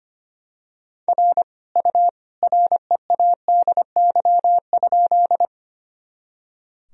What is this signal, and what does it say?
Morse code, heard as a keyed tone, about 25 wpm, 700 Hz. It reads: R U READY?
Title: Sentences from Top 400 Words and Abbreviations 25wpm